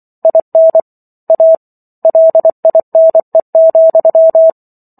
Morse code, heard as a keyed tone, about 24 wpm, 650 Hz. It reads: IN A LINE,